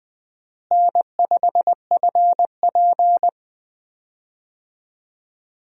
Morse code, heard as a keyed tone, about 21 wpm, 700 Hz. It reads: N5FP
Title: USA Call Signs - 4 Characters - Repeated 2x - 20wpm